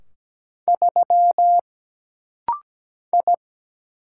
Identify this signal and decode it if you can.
Morse code, heard as a keyed tone, about 17 wpm, 700 Hz. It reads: 3 I
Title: Single Letter-Number - Rapid-Fire 17wpm